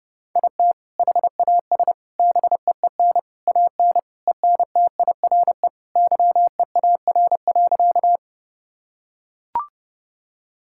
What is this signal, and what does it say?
Morse code, heard as a keyed tone, about 30 wpm, 700 Hz. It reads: IT HAS BEEN AN ENTIRE YEAR.
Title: Sentences from Top 1000 Words - Review Part 5 - 30wpm